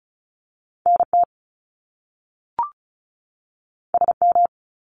Morse code, heard as a keyed tone, about 36 wpm, 700 Hz. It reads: NT SM
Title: Top Two Letter Combinations in Words 35wpm